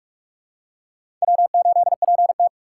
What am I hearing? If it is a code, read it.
Morse code, heard as a keyed tone, about 46 wpm, 690 Hz. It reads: W8PT
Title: USA Call Signs - 4 Characters - 8x Character Spacing - 45wpm